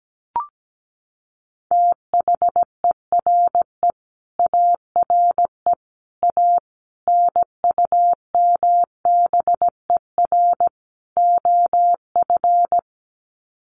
Morse code, heard as a keyed tone, about 17 wpm, 700 Hz. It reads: THERE ARE A NUMBER OF
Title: Sets of 5 Words 17wpm